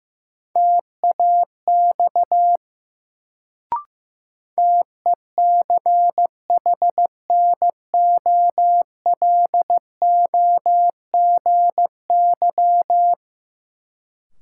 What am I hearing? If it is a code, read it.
Morse code, heard as a keyed tone, about 15 wpm, 700 Hz. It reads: TAX TECHNOLOGY
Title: Sentences from Top 1000 Words - Review Part 5 - New Word Drill - 15wpm